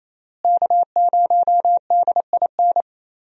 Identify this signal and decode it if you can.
Morse code, heard as a keyed tone, about 27 wpm, 700 Hz. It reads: K0DIN
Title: USA Call Signs - 5 Characters - Repeated 3x - 28wpm